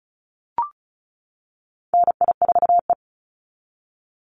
Morse code, heard as a keyed tone, about 34 wpm, 700 Hz. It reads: NI4E
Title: Call Signs - 35wpm